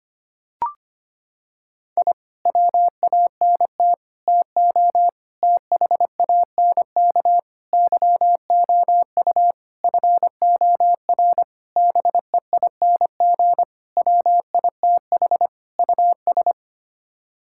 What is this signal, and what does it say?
Morse code, heard as a keyed tone, about 25 wpm, 700 Hz. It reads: I WANT TO THANK YOU FOR BEING WITH US
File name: Sets of 9 Words - 25wpm